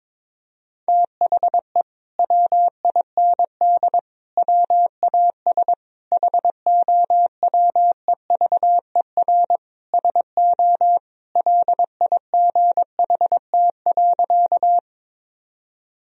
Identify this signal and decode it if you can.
Morse code, heard as a keyed tone, about 22 wpm, 700 Hz. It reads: THE WIND WAS HOWEVER SO LIGHT.
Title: Sentences from Top 1000 Words - Review Part 2 - 22wpm